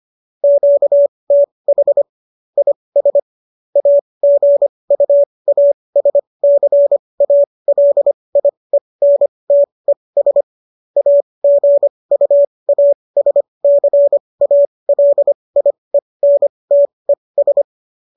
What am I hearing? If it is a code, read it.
Morse code, heard as a keyed tone, about 25 wpm, spaced as for 20 wpm, 570 Hz. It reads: QTH IS AGUASCALIENTES AGUASCALIENTES